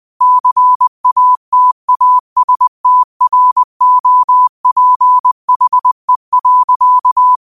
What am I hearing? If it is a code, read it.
Morse code, heard as a keyed tone, about 20 wpm, 1000 Hz. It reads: CATASTROPHE.
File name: qotdfiles / 20wpm